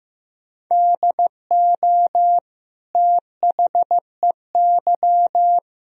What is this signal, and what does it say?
Morse code, heard as a keyed tone, about 15 wpm, 700 Hz. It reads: DO THEY
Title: Sentences from Top 1000 Words - Review Part 3 - 15wpm